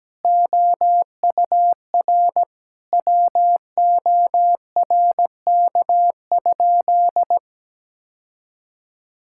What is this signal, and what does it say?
Morse code, heard as a keyed tone, about 17 wpm, 700 Hz. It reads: OUR WORK?
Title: Sentences from Top 1000 Words - Review Part 6 - 17wpm